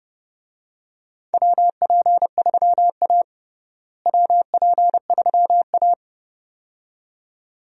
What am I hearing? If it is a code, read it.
Morse code, heard as a keyed tone, about 30 wpm, 700 Hz. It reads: WP3A WP3A
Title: USA Call Signs - 4 Characters - Repeated 2x - 30wpm